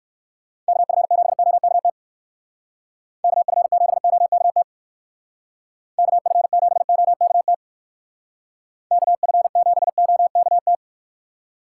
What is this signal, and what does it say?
Morse code, heard as a keyed tone, about 56 wpm, 700 Hz. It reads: KF7OKT KF7OKT KF7OKT KF7OKT
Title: Speed Racing - USA Call Signs 45wpm